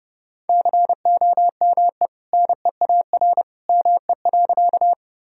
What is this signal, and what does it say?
Morse code, heard as a keyed tone, about 30 wpm, 700 Hz. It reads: COME NEAR ME.
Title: Sentences from Top 1000 Words - Review Part 1 - 30wpm